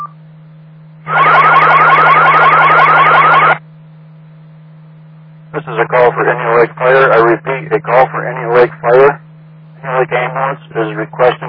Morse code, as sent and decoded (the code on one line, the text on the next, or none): none